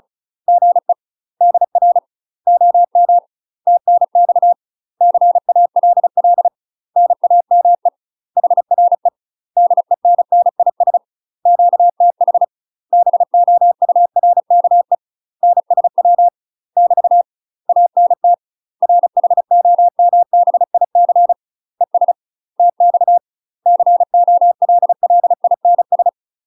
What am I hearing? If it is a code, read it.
Morse code, heard as a keyed tone, about 36 wpm, 700 Hz. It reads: GE DR OM TNX CALL NAME HRE DENNIS QTH BOURKE NSW <BT> ANT RHOMBIC ES TX COLLINS
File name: UK MASTER QSOs TRAINING / 35WPM / UK MASTER QSOs 0017